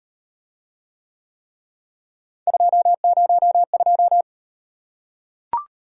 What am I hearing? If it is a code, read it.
Morse code, heard as a keyed tone, about 33 wpm, 700 Hz. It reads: 202